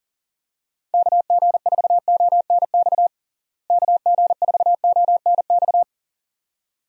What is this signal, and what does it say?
Morse code, heard as a keyed tone, about 39 wpm, 700 Hz. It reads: KG4ONX KG4ONX
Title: USA Call Signs - 6 Characters - Repeated 2x - 40wpm